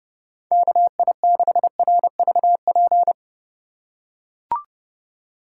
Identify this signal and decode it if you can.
Morse code, heard as a keyed tone, about 31 wpm, 700 Hz. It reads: KI6RVP
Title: USA Call Signs - 6 Characters - 4x Character Spacing - 30wpm